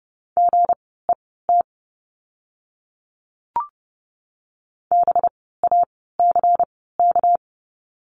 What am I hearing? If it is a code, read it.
Morse code, heard as a keyed tone, about 30 wpm, spaced as for 17 wpm, 700 Hz. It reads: GET BACK
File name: course / 30:10wpm / 237 Lesson 38f - Letter q - Words 30wpm